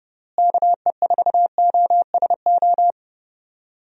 Morse code, heard as a keyed tone, about 31 wpm, 700 Hz. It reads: KE4OSO
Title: USA Call Signs - 6 Characters - Repeated 3x - 30wpm